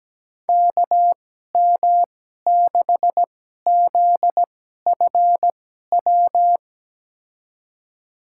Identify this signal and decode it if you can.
Morse code, heard as a keyed tone, about 17 wpm, spaced as for 12 wpm, 700 Hz. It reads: KM6ZFW
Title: USA Call Signs - 6 Characters - 2x Character Spacing - 17wpm